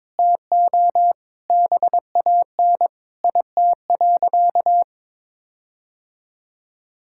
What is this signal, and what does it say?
Morse code, heard as a keyed tone, about 22 wpm, 700 Hz. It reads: TO BAN IT.
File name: Sentences with 7-Word Phrases - 22wpm